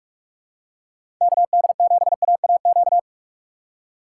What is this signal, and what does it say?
Morse code, heard as a keyed tone, about 46 wpm, 700 Hz. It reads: KD7AAQ
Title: USA Call Signs - 6 Characters - 4x Character Spacing - 45wpm